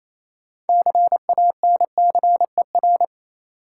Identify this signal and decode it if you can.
Morse code, heard as a keyed tone, about 28 wpm, 700 Hz. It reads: CANCER
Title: Six Letter Words - 28wpm